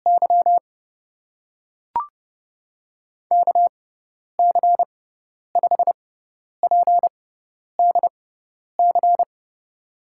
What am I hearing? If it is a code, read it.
Morse code, heard as a keyed tone, about 30 wpm, spaced as for 10 wpm, 700 Hz. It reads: Y KC5PDC
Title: USA Call Signs - 6x Character Spacing - 30wpm